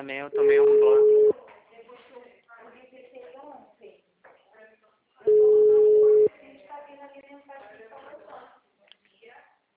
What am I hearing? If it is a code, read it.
Ringback tone (Europe, 425 Hz), with speech.